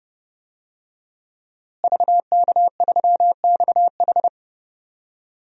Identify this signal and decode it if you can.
Morse code, heard as a keyed tone, about 32 wpm, 700 Hz. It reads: VK3XH
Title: DXCC-100 Call Signs - 5 Characters - Repeated 3x - 30wpm